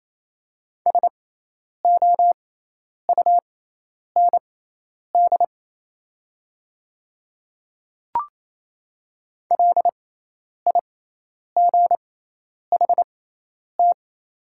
Morse code, heard as a keyed tone, about 28 wpm, spaced as for 10 wpm, 700 Hz. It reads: SOUND LIGHT
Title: 6x Character Spacing - Top 300 Words - 28wpm